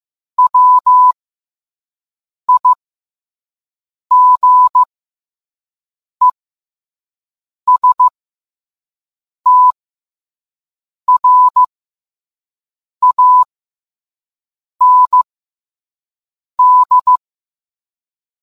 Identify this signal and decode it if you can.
Morse code, heard as a keyed tone, about 15 wpm, spaced as for 5 wpm, 1000 Hz. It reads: WIGESTRAND